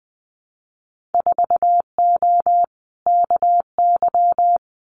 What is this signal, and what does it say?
Morse code, heard as a keyed tone, about 19 wpm, 700 Hz. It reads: 4O KY
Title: Winter Field Day Exchanges Encore 20wpm